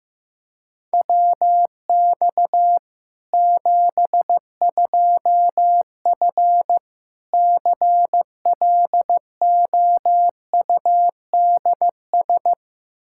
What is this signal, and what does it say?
Morse code, heard as a keyed tone, about 15 wpm, 700 Hz. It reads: WX 72F CLOUDS
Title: K1USN SST QSO Practice 15wpm